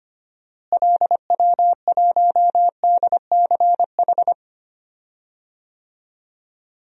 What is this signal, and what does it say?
Morse code, heard as a keyed tone, about 25 wpm, 700 Hz. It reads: LW1DCH
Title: DXCC-100 Call Signs - 6 Characters - Repeated 2x - 25wpm